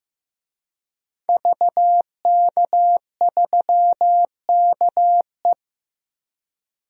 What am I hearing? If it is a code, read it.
Morse code, heard as a keyed tone, about 15 wpm, 700 Hz. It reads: VK3KE
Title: DXCC-100 Call Signs - 5 Characters - Repeated 3x - 15wpm